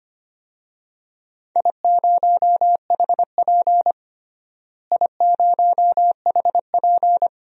Morse code, heard as a keyed tone, about 25 wpm, 700 Hz. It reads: I0HP I0HP
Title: DXCC-100 Call Signs - 4 Characters - Repeated 3x - 25wpm